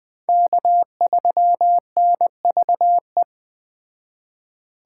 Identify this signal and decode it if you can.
Morse code, heard as a keyed tone, about 21 wpm, 700 Hz. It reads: K3NVE
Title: Speed Racing - USA 1x3 Call Signs 17wpm